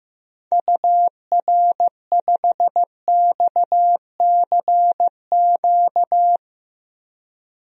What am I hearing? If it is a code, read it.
Morse code, heard as a keyed tone, about 15 wpm, 700 Hz. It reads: UR5XCQ
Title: DXCC-100 Call Signs - 6 Characters - 15wpm